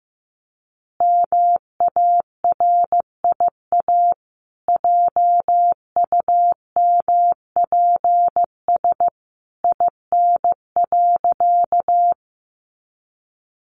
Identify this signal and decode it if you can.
Morse code, heard as a keyed tone, about 15 wpm, 700 Hz. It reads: MARIA JUMPS IN.
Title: Easy Phrases 15wpm